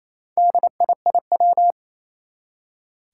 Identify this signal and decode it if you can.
Morse code, heard as a keyed tone, about 28 wpm, 700 Hz. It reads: DIIW
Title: Four Challenging Letter-Number Combinations 28wpm